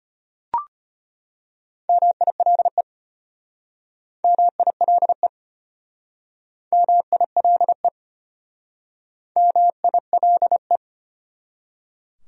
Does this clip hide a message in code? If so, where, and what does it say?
Yes: Morse code from 1.9 to 10.8 s, heard as a keyed tone, about 34 wpm, 700 Hz. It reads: MILE MILE MILE MILE